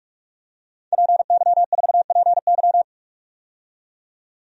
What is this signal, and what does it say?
Morse code, heard as a keyed tone, about 46 wpm, 700 Hz. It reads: PY4PY
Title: DXCC-100 Call Signs - 5 Characters - 45wpm